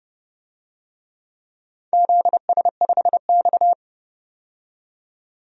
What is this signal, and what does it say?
Morse code, heard as a keyed tone, about 30 wpm, 700 Hz. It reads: ZS5X
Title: DXCC-100 Call Signs - 4 Characters - 30wpm